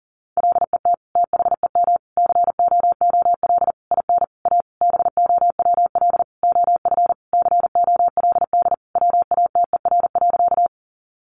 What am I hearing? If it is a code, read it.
Morse code, heard as a keyed tone, about 40 wpm, 700 Hz. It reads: LET THEM COOL IN A BOWL OF COLD WATER.
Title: Sentences from Top 1200 Words 40wpm